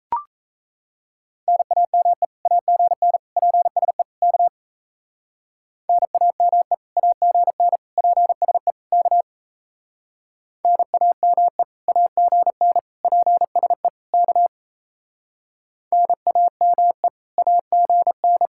Morse code, heard as a keyed tone, about 34 wpm, 700 Hz. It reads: NAME AGN PSE K NAME AGN PSE K NAME AGN PSE K NAME AGN
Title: Speed-Racing Common QSO Phrases - 28wpm